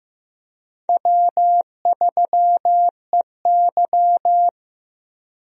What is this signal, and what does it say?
Morse code, heard as a keyed tone, about 15 wpm, 700 Hz. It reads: W3EY
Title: USA Call Signs - 4 Characters - Repeated 2x - 15wpm